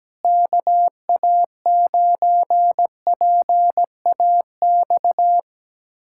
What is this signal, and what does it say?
Morse code, heard as a keyed tone, about 17 wpm, 700 Hz. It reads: KA9PAX
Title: USA Callsign DE Callsign - 6 Characters - 17wpm